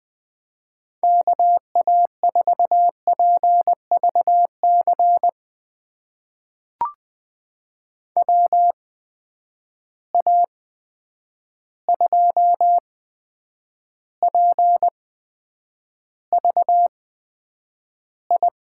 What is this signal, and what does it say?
Morse code, heard as a keyed tone, about 20 wpm, spaced as for 9 wpm, 700 Hz. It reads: KA4PVC WA2PVI
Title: USA Call Signs - 6 Characters - 8x Character Spacing - 20wpm